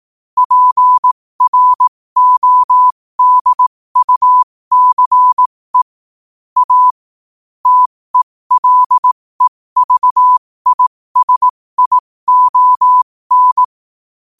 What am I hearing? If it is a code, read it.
Morse code, heard as a keyed tone, about 18 wpm, spaced as for 15 wpm, 1000 Hz. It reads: PRODUCE A TELEVISION
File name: qotdfiles / 15wpm